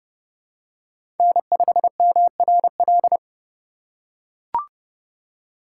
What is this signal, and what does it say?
Morse code, heard as a keyed tone, about 30 wpm, 700 Hz. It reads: N5MRL
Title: USA Call Signs - 5 Characters - 2x Character Spacing - 30wpm